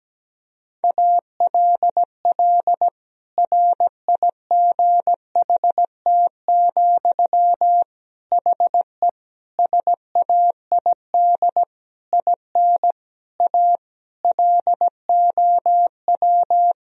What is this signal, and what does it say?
Morse code, heard as a keyed tone, about 17 wpm, 700 Hz. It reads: ALL RIGHT, HE SAID IN A LOW
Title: Sentences from Top 1000 Words - Review Part 1 - 17wpm